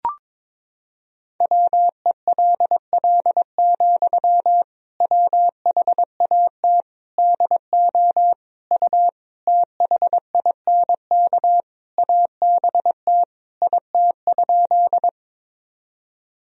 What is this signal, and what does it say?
Morse code, heard as a keyed tone, about 22 wpm, 700 Hz. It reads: WELL, WHAT DO U THINK ABT IT?
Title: Sentences from Top 200 Words and Abbreviations 22wpm